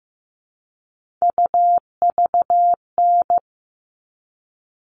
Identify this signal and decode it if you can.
Morse code, heard as a keyed tone, about 15 wpm, 700 Hz. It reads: UVN
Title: All Cut Numbers - 3-Digit 15wpm